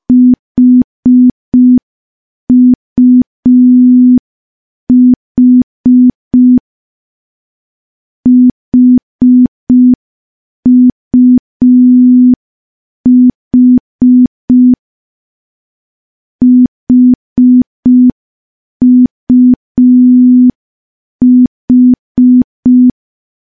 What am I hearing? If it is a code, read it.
Morse code, heard as a keyed tone, about 5 wpm, 250 Hz. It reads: HUH HUH HUH
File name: morse / morse-training-05WPM